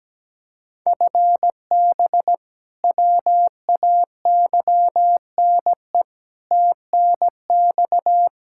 Morse code, heard as a keyed tone, about 17 wpm, 700 Hz. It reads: FB WAYNE TNX
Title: K1USN SST QSO Practice 17wpm